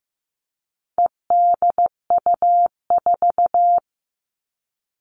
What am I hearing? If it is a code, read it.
Morse code, heard as a keyed tone, about 15 wpm, 700 Hz. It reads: EDU4
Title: All Cut Numbers - 4-Digit 15wpm